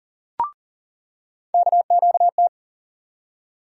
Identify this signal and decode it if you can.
Morse code, heard as a keyed tone, about 38 wpm, 700 Hz. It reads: KQT
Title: USA Call Sign Postfixes - 3 Letters 40wpm